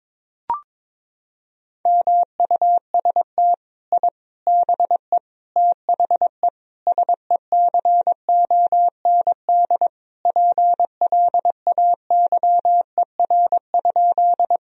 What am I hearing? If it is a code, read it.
Morse code, heard as a keyed tone, about 21 wpm, 700 Hz. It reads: MUST I BE THE SECOND PLAYER?
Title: Sentences from Top 1000 Words - Review Part 4 - 22wpm